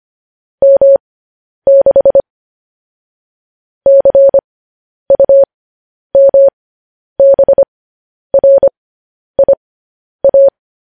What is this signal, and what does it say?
Morse code, heard as a keyed tone, about 25 wpm, spaced as for 10 wpm, 550 Hz. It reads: M6 CUMBRIA
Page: news headlines in morse code at 10 wpm